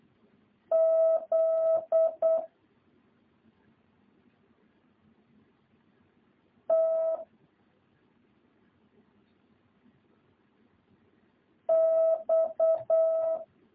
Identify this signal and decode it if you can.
Morse code, heard as a keyed tone, about 8 wpm, 650 Hz. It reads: Z T X